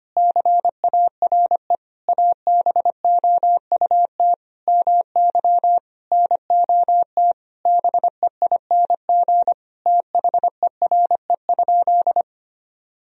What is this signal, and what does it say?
Morse code, heard as a keyed tone, about 25 wpm, 700 Hz. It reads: CARE ABOUT MY NOT BEING THERE?
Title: Sentences from Top 1000 Words - Review Part 3 - 25wpm